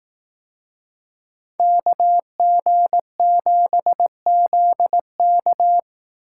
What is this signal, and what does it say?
Morse code, heard as a keyed tone, about 18 wpm, 700 Hz. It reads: KG7ZK
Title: Speed Racing - USA 2x2 Call Signs 15wpm